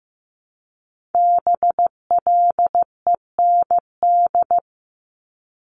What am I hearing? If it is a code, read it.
Morse code, heard as a keyed tone, about 15 wpm, 700 Hz. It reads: BLEND
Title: Multi-letter Phonemes Repeated 3x 15wpm